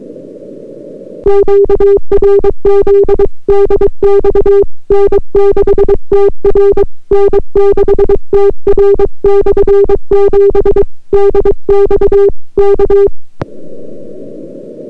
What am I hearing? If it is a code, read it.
Morse code, heard as a keyed tone, about 22 wpm, 400 Hz. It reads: QRZ DX N6TR N6TR/7 DX K